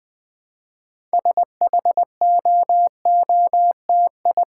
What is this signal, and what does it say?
Morse code, heard as a keyed tone, about 20 wpm, 700 Hz. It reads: SHOOTI
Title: Build A Word - 2 Letters At A Time - 8-Letter Words - 20wpm